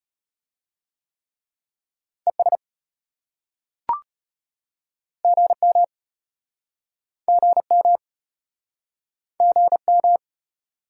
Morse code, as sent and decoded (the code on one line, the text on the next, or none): . ... / --. -- / --. -- / --. --
ES GM GM GM